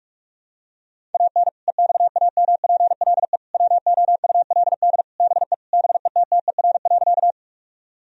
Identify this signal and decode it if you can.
Morse code, heard as a keyed tone, about 46 wpm, 700 Hz. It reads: AN EXAMPLE WOULD BE BETTER.
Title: Sentences from Top 1000 Words - Review Part 1 - 45wpm